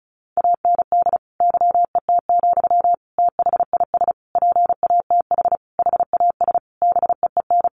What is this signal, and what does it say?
Morse code, heard as a keyed tone, about 34 wpm, 700 Hz. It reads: AND YET, THIS PATH HAS BEEN
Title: Sentences from Top 1400 Words 35wpm